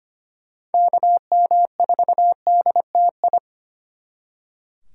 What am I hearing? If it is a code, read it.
Morse code, heard as a keyed tone, about 33 wpm, 700 Hz. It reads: KM4DTI